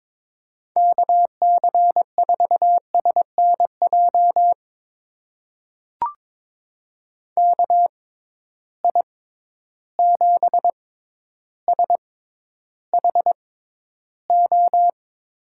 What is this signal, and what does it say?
Morse code, heard as a keyed tone, about 22 wpm, spaced as for 11 wpm, 700 Hz. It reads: KC4SNJ KI7SHO
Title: USA Call Signs - 6 Characters - 6x Character Spacing - 22wpm